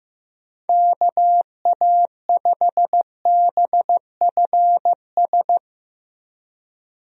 Morse code, heard as a keyed tone, about 15 wpm, 700 Hz. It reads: KA5BFS
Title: USA Call Signs - 6 Characters - Repeated 2x - 15wpm